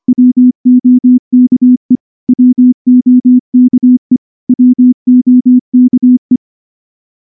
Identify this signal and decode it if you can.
Morse code, heard as a keyed tone, about 24 wpm, 250 Hz. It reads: WOKE WOKE WOKE
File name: morse / morse-training-25WPM